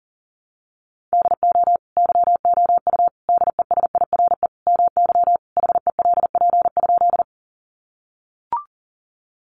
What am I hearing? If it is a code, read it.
Morse code, heard as a keyed tone, about 40 wpm, 700 Hz. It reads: DO YOU DESIRE MY HELP?
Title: Sentences from Top 1100 Words 40wpm